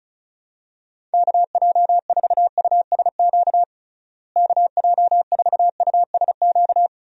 Morse code, heard as a keyed tone, about 36 wpm, 700 Hz. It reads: KJ4USQ KJ4USQ
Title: USA Call Signs - 6 Characters - Repeated 2x - 35wpm